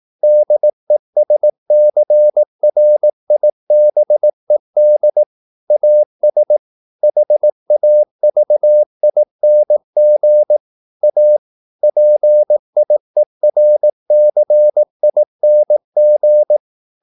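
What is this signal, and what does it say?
Morse code, heard as a keyed tone, about 18 wpm, 600 Hz. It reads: DESCRIBED AS HAVING A PIERCING
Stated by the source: Morse practice at 18 wpm, English text, from CW med Gnister.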